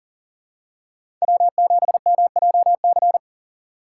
Morse code, heard as a keyed tone, about 40 wpm, 700 Hz. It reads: W7MJC
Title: USA Call Signs - 5 Characters - 3x Character Spacing - 40wpm